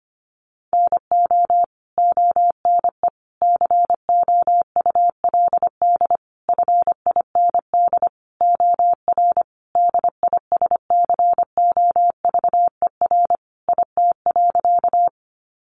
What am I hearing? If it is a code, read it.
Morse code, heard as a keyed tone, about 25 wpm, 700 Hz. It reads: NO ONE COULD FIND OR DISCOVER IT.
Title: Sentences from Top 1100 Words 25wpm